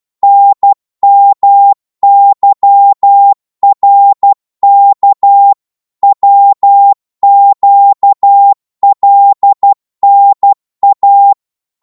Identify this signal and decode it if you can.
Morse code, heard as a keyed tone, about 12 wpm, 800 Hz. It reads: NMYRK WQLNA
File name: cwmp3 / ltr-12wpm